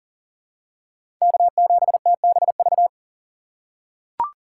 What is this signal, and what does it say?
Morse code, heard as a keyed tone, about 40 wpm, 700 Hz. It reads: K7TBV
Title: USA Call Signs - 5 Characters - 2x Character Spacing - 40wpm